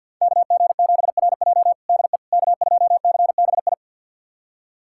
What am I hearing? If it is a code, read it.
Morse code, heard as a keyed tone, about 52 wpm, 700 Hz. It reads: KG7RJ DE K2CBI
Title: USA Callsign DE Callsign - 5 Characters - 50wpm